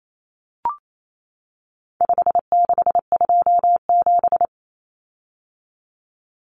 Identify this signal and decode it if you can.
Morse code, heard as a keyed tone, about 28 wpm, 700 Hz. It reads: 5627
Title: Common Cut Numbers - 4-Digit 28wpm